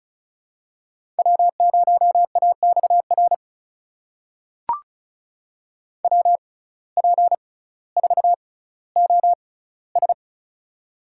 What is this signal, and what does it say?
Morse code, heard as a keyed tone, about 36 wpm, spaced as for 18 wpm, 700 Hz. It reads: W0AXR WP4OS
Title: USA Call Signs - 5 Characters - 6x Character Spacing - 35wpm